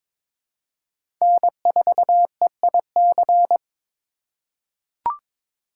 Morse code, heard as a keyed tone, about 21 wpm, 700 Hz. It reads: N4EIC
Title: USA Call Signs - 5 Characters - 6x Character Spacing - 22wpm